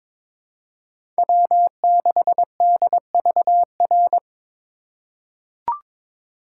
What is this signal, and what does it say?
Morse code, heard as a keyed tone, about 22 wpm, 700 Hz. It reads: W6DVR
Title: USA Call Signs - 5 Characters - 4x Character Spacing - 22wpm